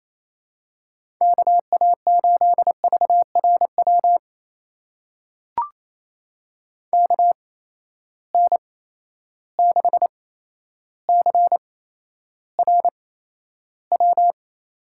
Morse code, heard as a keyed tone, about 28 wpm, spaced as for 12 wpm, 700 Hz. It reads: KA8VRW KN6CRW
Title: USA Call Signs - 6 Characters - 8x Character Spacing - 28wpm